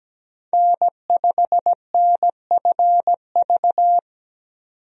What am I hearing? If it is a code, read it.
Morse code, heard as a keyed tone, about 17 wpm, 700 Hz. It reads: N5NFV
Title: USA Call Signs - 5 Characters - 17wpm